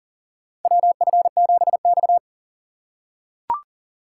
Morse code, heard as a keyed tone, about 40 wpm, 700 Hz. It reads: WF7X
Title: USA Call Signs - 4 Characters - 6x Character Spacing - 40wpm